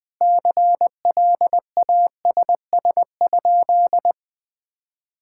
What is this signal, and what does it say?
Morse code, heard as a keyed tone, about 21 wpm, 700 Hz. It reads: CLASS?
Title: Sentences from Top 1000 Words - Review Part 6 - 20wpm